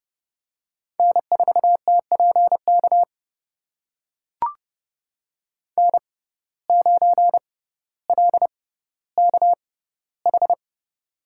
Morse code, heard as a keyed tone, about 30 wpm, spaced as for 15 wpm, 700 Hz. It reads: N4TPK N9LKH
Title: USA Call Signs - 5 Characters - 6x Character Spacing - 30wpm